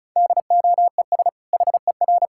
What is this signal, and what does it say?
Morse code, heard as a keyed tone, about 36 wpm, 700 Hz. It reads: DOES HER
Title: Sentences from Top 1000 Words - Review Part 5 - 35wpm